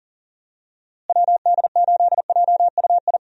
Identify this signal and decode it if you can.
Morse code, heard as a keyed tone, about 40 wpm, 700 Hz. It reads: WD8JUI
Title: USA Call Signs - 6 Characters - Repeated 3x - 40wpm